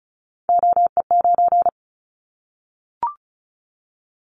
Morse code, heard as a keyed tone, about 34 wpm, 700 Hz. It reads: OE9
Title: Callsign Prefixes for Top 100 DXCC - 35wpm